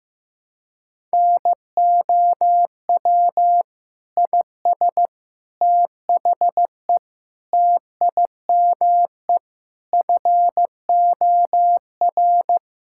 Morse code, heard as a keyed tone, about 16 wpm, 700 Hz. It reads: NOW IS THE TIME FOR
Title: Pangram Challenge v2 - 15wpm